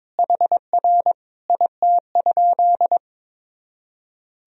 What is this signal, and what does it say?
Morse code, heard as a keyed tone, about 22 wpm, 700 Hz. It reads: HR IT?
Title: Sentences from Top 300 Words and Abbreviations 22wpm